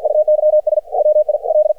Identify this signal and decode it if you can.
Morse code, heard as a keyed tone, about 54 wpm, 610 Hz. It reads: UYS WIL
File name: archived / CRSnet